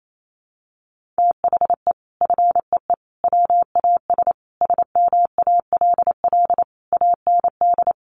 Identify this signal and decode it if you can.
Morse code, heard as a keyed tone, about 28 wpm, 700 Hz. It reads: THE FEE WAS SMALL AND
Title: Sentences from Top 1500 Words 28wpm